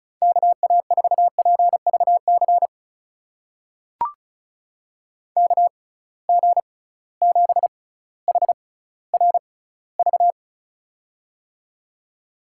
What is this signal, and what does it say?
Morse code, heard as a keyed tone, about 36 wpm, spaced as for 18 wpm, 700 Hz. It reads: KA4PVC KG7HRV
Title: USA Call Signs - 6 Characters - 6x Character Spacing - 35wpm